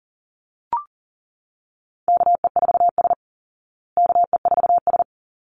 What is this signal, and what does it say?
Morse code, heard as a keyed tone, about 40 wpm, 700 Hz. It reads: KE4S KE4S
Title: Call Signs Repeated 3x - 40wpm